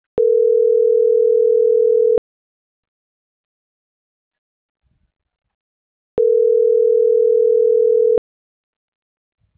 Ringback tone (North America), 440 + 480 Hz.
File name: CENTRAL NAVIERAS / out onelinecapacitacion - .78699